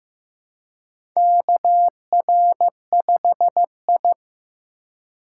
Morse code, heard as a keyed tone, about 15 wpm, 700 Hz. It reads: KR5I
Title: USA Call Signs - 2x Character Spacing - 15wpm